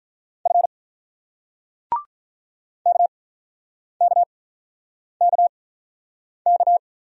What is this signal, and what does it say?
Morse code, heard as a keyed tone, about 47 wpm, spaced as for 9 wpm, 690 Hz. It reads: F KKKK